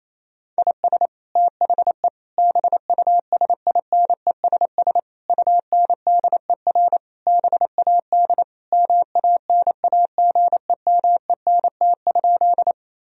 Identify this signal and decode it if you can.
Morse code, heard as a keyed tone, about 28 wpm, 700 Hz. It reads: IS THE BUSINESS UNDER BAD MANAGEMENT?
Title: Sentences from Top 1000 Words - Review Part 4 - 28wpm